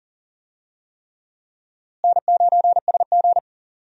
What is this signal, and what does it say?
Morse code, heard as a keyed tone, about 40 wpm, 700 Hz. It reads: N9SG
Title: USA Call Signs - 4 Characters - Repeated 3x - 40wpm